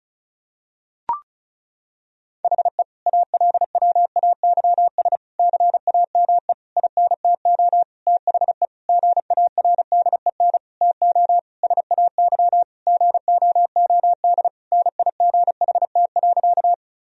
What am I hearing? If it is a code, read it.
Morse code, heard as a keyed tone, about 36 wpm, 700 Hz. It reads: HE ALWAYS CAME INTO THE GARDEN TO SAY GOOD NIGHT.
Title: Sentences from Top 1000 Words - Review Part 4 - 35wpm